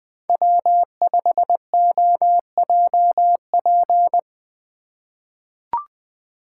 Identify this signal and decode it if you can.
Morse code, heard as a keyed tone, about 21 wpm, 700 Hz. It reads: W5OJP
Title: USA Call Signs - 5 Characters - 3x Character Spacing - 20wpm